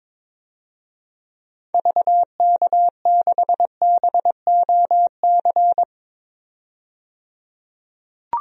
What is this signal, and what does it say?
Morse code, heard as a keyed tone, about 22 wpm, 700 Hz. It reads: VK6BOC
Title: DXCC-100 Call Signs - 6 Characters - 22wpm